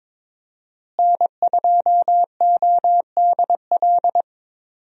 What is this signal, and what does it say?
Morse code, heard as a keyed tone, about 21 wpm, 700 Hz. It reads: N2ODL
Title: USA Call Signs - 5 Characters - Repeated 2x - 22wpm